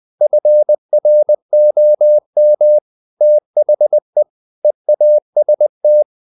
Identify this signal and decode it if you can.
Morse code, heard as a keyed tone, about 20 wpm, 600 Hz. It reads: FROM THE EAST
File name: the old man and the sea / the old man and the sea part 1 20wpm